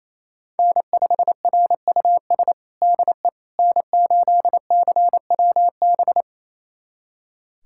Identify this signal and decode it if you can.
Morse code, heard as a keyed tone, about 27 wpm, 700 Hz. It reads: N5RUS DE N8CWB
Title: USA Callsign DE Callsign - 5 Characters - 28wpm